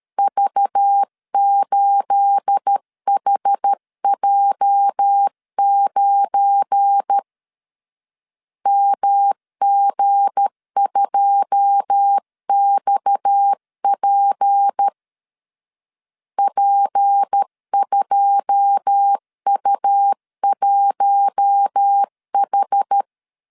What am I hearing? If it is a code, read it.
Morse code, heard as a keyed tone, about 13 wpm, 800 Hz. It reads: V8HJ9 MG2XP P2U1H
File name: LEZIONI / 9 28 12wpm 2351